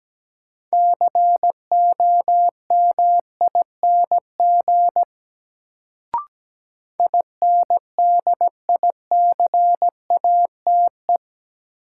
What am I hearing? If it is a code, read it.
Morse code, heard as a keyed tone, about 17 wpm, 700 Hz. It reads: COMING INDICATE
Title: Sentences from Top 1300 Words - New Word Drill Alt - 17wpm